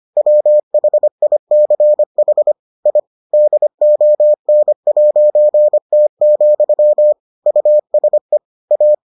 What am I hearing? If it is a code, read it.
Morse code, heard as a keyed tone, about 25 wpm, 600 Hz. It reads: WHICH I DON'T, USE A